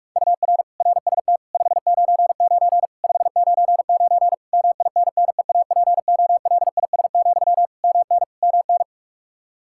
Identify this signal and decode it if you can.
Morse code, heard as a keyed tone, about 41 wpm, 700 Hz. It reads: UR RST 599 599 MINNEAPOLIS, MN MN
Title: Practice QSO - Exchange 1 - Beginning - 45wpm